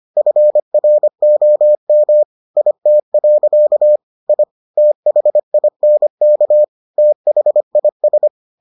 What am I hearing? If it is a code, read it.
Morse code, heard as a keyed tone, about 25 wpm, 600 Hz. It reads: FROM IT. I THINK THIS